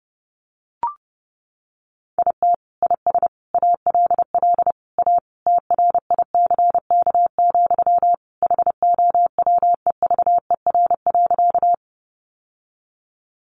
Morse code, heard as a keyed tone, about 30 wpm, 700 Hz. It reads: IT IS ALL A TRICK, HOWEVER.
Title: Sentences from Top 1500 Words 30wpm